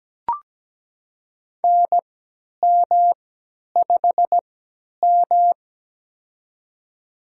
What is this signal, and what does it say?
Morse code, heard as a keyed tone, about 17 wpm, spaced as for 10 wpm, 700 Hz. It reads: NM5M
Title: USA Call Signs - 4 Characters - 3x Character Spacing - 17wpm